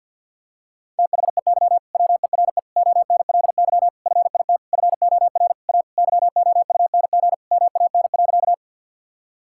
Morse code, heard as a keyed tone, about 50 wpm, 700 Hz. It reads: THEY WERE ONLY FIT FOR A YOUNG MAN.
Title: Sentences from Top 1000 Words - Review Part 5 - 50wpm